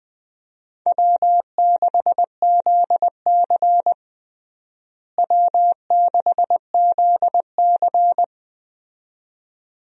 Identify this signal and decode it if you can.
Morse code, heard as a keyed tone, about 20 wpm, 700 Hz. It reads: W6ZC W6ZC
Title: USA Call Signs - 4 Characters - Repeated 2x - 20wpm